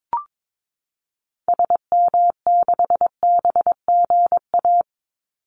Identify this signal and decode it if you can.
Morse code, heard as a keyed tone, about 22 wpm, 700 Hz. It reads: SM6BGA
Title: Call Signs Repeated 2x - 22wpm